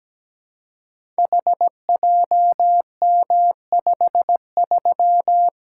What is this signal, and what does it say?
Morse code, heard as a keyed tone, about 17 wpm, 700 Hz. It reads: HJM53
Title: Code Groups - 5 Letter:Numbers - 17wpm